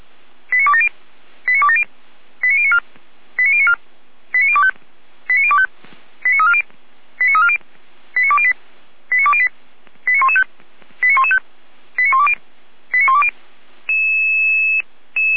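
Five-tone selective call (ZVEI), heard as quick runs of five tones, tones of about 70 ms. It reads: 89270 89270 89094 89094 89025 89025 89340 89340 89208 89208 89115 89115 89120 89120
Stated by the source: ZVEI five-tone selective call 89358 received on SDR.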